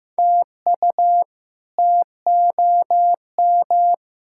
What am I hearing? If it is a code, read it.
Morse code, heard as a keyed tone, about 15 wpm, 700 Hz. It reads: TU TOM
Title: K1USN SST QSO Practice - 15wpm